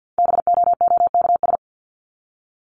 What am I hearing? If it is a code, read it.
Morse code, heard as a keyed tone, about 50 wpm, 700 Hz. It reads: BOOKS
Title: ES:S Suffix - Top 100 Words with Base Words 50wpm